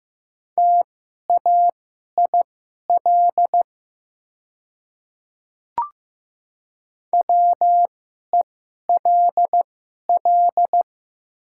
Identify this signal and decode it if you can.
Morse code, heard as a keyed tone, about 15 wpm, spaced as for 11 wpm, 690 Hz. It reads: TAIL WELL
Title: 2x Character Spacing - Top 500 Words - 15wpm